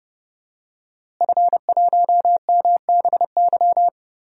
Morse code, heard as a keyed tone, about 30 wpm, 700 Hz. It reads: F1MBY
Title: DXCC-100 Call Signs - 5 Characters - Repeated 3x - 30wpm